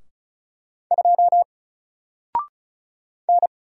Morse code, heard as a keyed tone, about 35 wpm, 700 Hz. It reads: 2 N
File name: Single-Letter-Number-35wpm